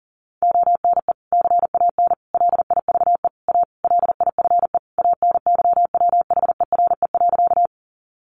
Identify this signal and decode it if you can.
Morse code, heard as a keyed tone, about 40 wpm, 700 Hz. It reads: ONE CAN LIVE A LIFE ANYWHERE.
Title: Sentences from Top 1400 Words 40wpm